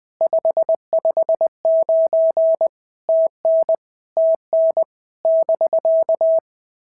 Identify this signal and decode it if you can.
Morse code, heard as a keyed tone, about 18 wpm, 650 Hz. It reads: 559 TN TN <BK>
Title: POTA QSO Practice - 20wpm